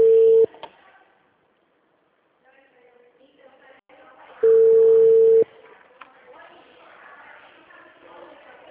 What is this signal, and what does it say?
Ringback tone (China, 450 Hz).
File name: out- -2002